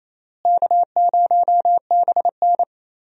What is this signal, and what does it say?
Morse code, heard as a keyed tone, about 28 wpm, 700 Hz. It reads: K0BN
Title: USA Call Signs - 4 Characters - Repeated 3x - 28wpm